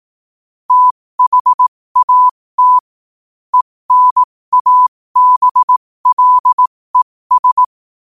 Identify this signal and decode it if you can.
Morse code, heard as a keyed tone, about 18 wpm, spaced as for 15 wpm, 1000 Hz. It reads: THAT ENABLES
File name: 2024-11-18-15wpm